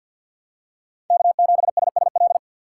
Morse code, heard as a keyed tone, about 52 wpm, 700 Hz. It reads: K7SSL
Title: USA Call Signs - 5 Characters - 4x Character Spacing - 50wpm